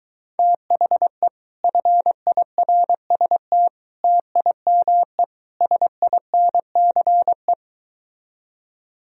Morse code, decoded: THE FIRST TIME SINCE

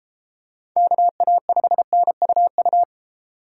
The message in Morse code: KA5NUU

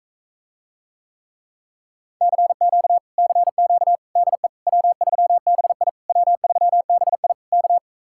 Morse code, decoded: CQ CQ DE W3BI W3BI K

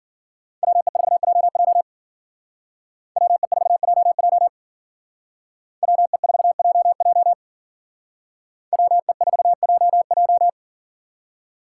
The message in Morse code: WE4JJ WE4JJ WE4JJ WE4JJ

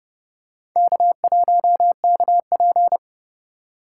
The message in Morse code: K1KP

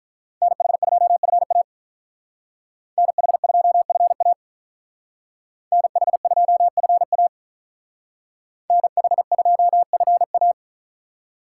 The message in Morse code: NH2FA NH2FA NH2FA NH2FA

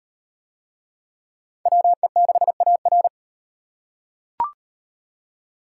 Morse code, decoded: WE6AR

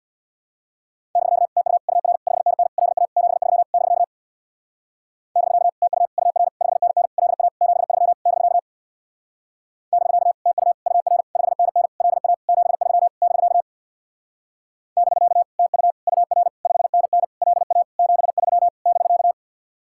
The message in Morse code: <BK> TU UR 5NN LA 73 <BK> <BK> TU UR 5NN LA 73 <BK> <BK> TU UR 5NN LA 73 <BK> <BK> TU UR 5NN LA 73 <BK>

